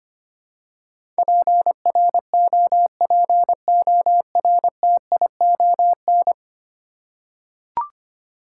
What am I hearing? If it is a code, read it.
Morse code, heard as a keyed tone, about 25 wpm, 700 Hz. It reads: PROPORTION
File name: practice / 25wpm / Ten Letter Words - 25wpm